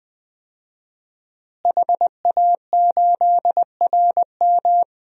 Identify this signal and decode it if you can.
Morse code, heard as a keyed tone, about 20 wpm, 700 Hz. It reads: HA8RM
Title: DXCC-100 Call Signs - 5 Characters - Repeated 2x - 20wpm